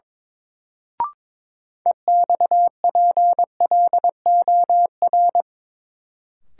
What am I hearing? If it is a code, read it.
Morse code, heard as a keyed tone, about 22 wpm, 700 Hz. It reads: EXPLOR
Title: Sentences from Top 1100 Words - New Word Drill - 22wpm